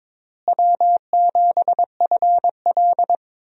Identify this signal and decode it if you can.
Morse code, heard as a keyed tone, about 22 wpm, 700 Hz. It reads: W7FL
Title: USA Call Signs - 4 Characters - Repeated 2x - 22wpm